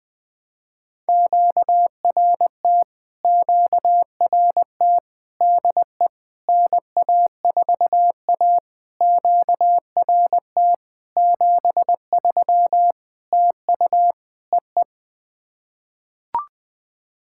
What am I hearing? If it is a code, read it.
Morse code, heard as a keyed tone, about 20 wpm, 700 Hz. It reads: QRT QRT DE NA4A QRT 73 TU EE
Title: POTA QSO Practice - 20wpm